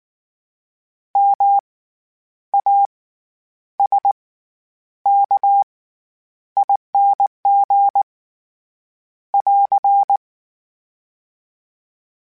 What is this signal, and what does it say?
Morse code, heard as a keyed tone, about 19 wpm, spaced as for 9 wpm, 800 Hz. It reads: MASKING <AR>
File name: Special INGs / Special INGs-mp3s / SlowToFaster ing @18